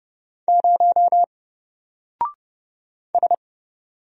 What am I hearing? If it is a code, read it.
Morse code, heard as a keyed tone, about 30 wpm, 700 Hz. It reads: 0 S
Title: Single Character (Letter:Number:Punctuation:Prosign) 30wpm